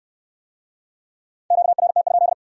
Morse code, heard as a keyed tone, about 64 wpm, 700 Hz. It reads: QRT?